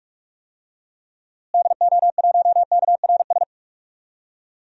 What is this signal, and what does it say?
Morse code, heard as a keyed tone, about 46 wpm, 700 Hz. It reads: DO1KRS
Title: DXCC-100 Call Signs - 6 Characters - Repeated 3x - 45wpm